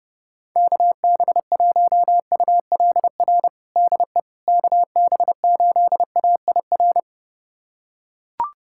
Morse code, heard as a keyed tone, about 30 wpm, 700 Hz. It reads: KB1ULR DE KB8AIR
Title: USA Callsign DE Callsign 30wpm